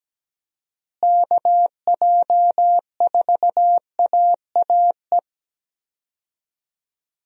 Morse code, heard as a keyed tone, about 17 wpm, 700 Hz. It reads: KJ4AAE
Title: USA Call Signs - 6 Characters - Repeated 2x - 17wpm